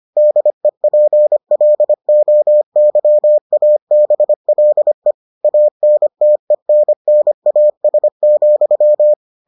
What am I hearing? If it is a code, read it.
Morse code, heard as a keyed tone, about 25 wpm, 600 Hz. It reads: DEPLOYABLE ANTENNAS,